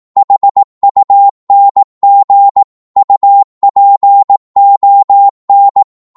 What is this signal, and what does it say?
Morse code, heard as a keyed tone, about 18 wpm, 800 Hz. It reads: HUNG UPON
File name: Morse / psg-18wpm